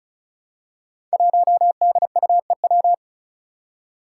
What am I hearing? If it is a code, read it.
Morse code, heard as a keyed tone, about 36 wpm, 700 Hz. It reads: 1DUEW
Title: Five Challenging Letter-Number Combinations 35wpm